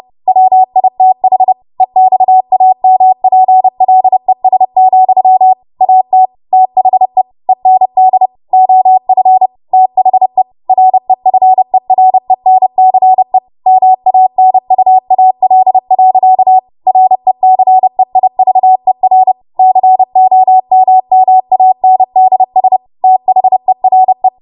Morse code, heard as a keyed tone, about 30 wpm, 750 Hz. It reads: WITH EXAMPLES, AT THE END OF THE REFERENCE MANUAL. RECEIVER COMMANDS THERE
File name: Morse / 30WPM